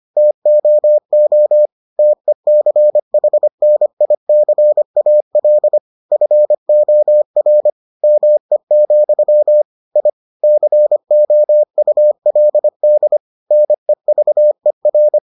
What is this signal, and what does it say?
Morse code, heard as a keyed tone, about 25 wpm, 600 Hz. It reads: TOO TECHNICAL FOR ME, I COULD NEVER